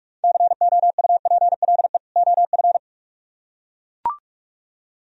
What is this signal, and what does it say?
Morse code, heard as a keyed tone, about 44 wpm, 700 Hz. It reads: COUPLE OF